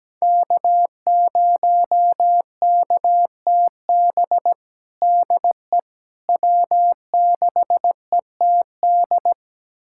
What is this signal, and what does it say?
Morse code, heard as a keyed tone, about 17 wpm, 700 Hz. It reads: K0KTB DE W6ETD
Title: USA Callsign DE Callsign - 5 Characters - 17wpm